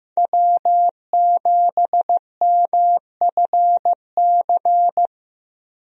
Morse code, heard as a keyed tone, about 16 wpm, 700 Hz. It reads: W7MFC
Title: USA Call Signs - 5 Characters - Repeated 3x - 15wpm